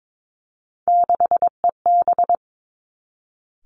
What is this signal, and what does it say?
Morse code, heard as a keyed tone, about 22 wpm, 700 Hz. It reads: 6EB
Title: Three Challenging Letter-Number Combinations 22wpm